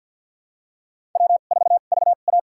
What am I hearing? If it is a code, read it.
Morse code, heard as a keyed tone, about 52 wpm, spaced as for 36 wpm, 700 Hz. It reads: W4VA
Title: USA Call Signs - 4 Characters - 2x Character Spacing - 50wpm